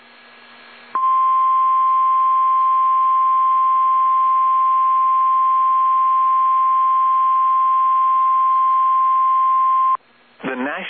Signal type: Emergency Alert System (SAME) tones